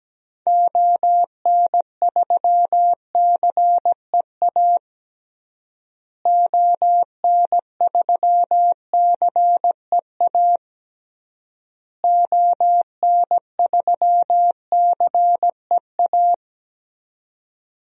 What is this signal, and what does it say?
Morse code, heard as a keyed tone, about 17 wpm, 700 Hz. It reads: ON3CEA ON3CEA ON3CEA
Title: DXCC-100 Call Signs - 6 Characters - Repeated 3x - 17wpm